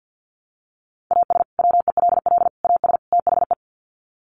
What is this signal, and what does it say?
Morse code, heard as a keyed tone, about 50 wpm, 700 Hz. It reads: AS WELL AS THE
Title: Sets of 4 Words 50wpm